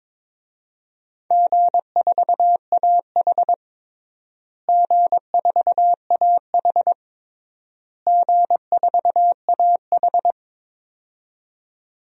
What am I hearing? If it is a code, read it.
Morse code, heard as a keyed tone, about 22 wpm, 700 Hz. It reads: G4AH G4AH G4AH
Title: DXCC-100 Call Signs - 4 Characters - Repeated 3x - 22wpm